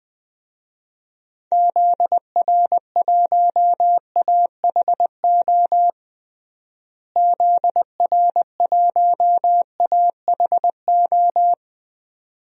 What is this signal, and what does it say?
Morse code, heard as a keyed tone, about 20 wpm, 700 Hz. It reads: ZR1AHO ZR1AHO
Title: DXCC-100 Call Signs - 6 Characters - Repeated 2x - 20wpm